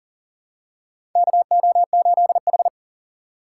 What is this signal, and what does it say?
Morse code, heard as a keyed tone, about 40 wpm, 700 Hz. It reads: KO8H
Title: USA Call Signs - 4 Characters - 3x Character Spacing - 40wpm